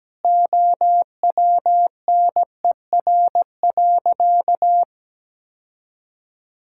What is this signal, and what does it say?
Morse code, heard as a keyed tone, about 17 wpm, 700 Hz. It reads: OWNER.
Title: Sentences from Top 1600 Words 17wpm